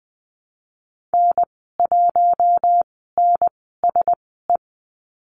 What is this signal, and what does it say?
Morse code, heard as a keyed tone, about 20 wpm, spaced as for 14 wpm, 700 Hz. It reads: N1NSE
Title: 042 Lesson 8g - Number 1 - Call Signs 20wpm